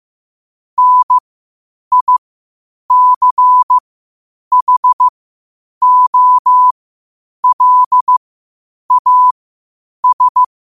Morse code, heard as a keyed tone, about 15 wpm, spaced as for 8 wpm, 1000 Hz. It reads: NICHOLAS
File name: qotdfiles / 8wpm